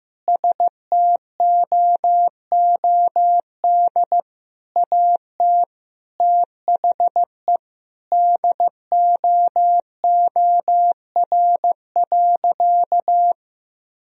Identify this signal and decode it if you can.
Morse code, heard as a keyed tone, about 15 wpm, 700 Hz. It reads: STOOD AT THE DOOR.
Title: Sentences from Top 1000 Words - Review Part 2 - 15wpm